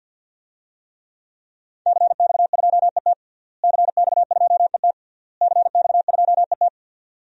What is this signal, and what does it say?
Morse code, heard as a keyed tone, about 52 wpm, 700 Hz. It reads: CX2ET CX2ET CX2ET
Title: DXCC-100 Call Signs - 5 Characters - Repeated 3x - 50wpm